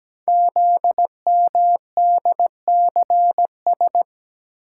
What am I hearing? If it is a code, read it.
Morse code, heard as a keyed tone, about 17 wpm, 700 Hz. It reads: ZMDCS
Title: Code Groups - 5 Letters - 17wpm